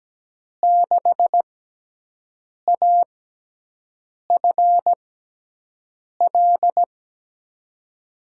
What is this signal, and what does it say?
Morse code, heard as a keyed tone, about 17 wpm, spaced as for 6 wpm, 700 Hz. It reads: 6AFL
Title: USA Call Signs - 6x Character Spacing - 17wpm